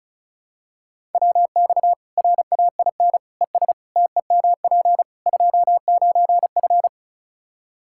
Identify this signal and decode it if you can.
Morse code, heard as a keyed tone, about 34 wpm, 700 Hz. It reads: WX RAIN ES TEMP 29F
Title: Practice QSO - Exchange 2 - Info - 35wpm